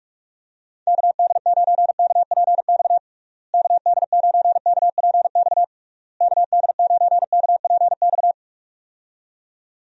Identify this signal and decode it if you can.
Morse code, heard as a keyed tone, about 42 wpm, 700 Hz. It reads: KD9KPX KD9KPX KD9KPX
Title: USA Call Signs - 6 Characters - Repeated 3x - 45wpm